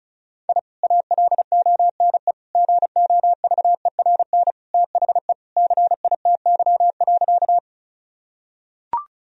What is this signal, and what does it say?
Morse code, heard as a keyed tone, about 34 wpm, 700 Hz. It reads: I ALONE GOVERN THE CITY.
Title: Sentences from Top 1000 Words - Review Part 2 - 35wpm